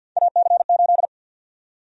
Morse code, heard as a keyed tone, about 50 wpm, 700 Hz. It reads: AC8
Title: USA Call Sign Prefixes - Repeated 2x - 50wpm